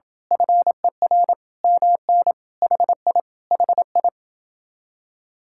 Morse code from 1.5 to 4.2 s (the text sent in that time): MN HI HI